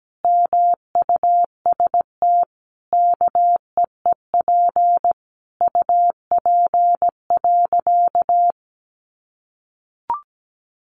Morse code, heard as a keyed tone, about 17 wpm, 700 Hz. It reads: MUST KEEP UP.
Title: Sentences from Top 700 Words 17wpm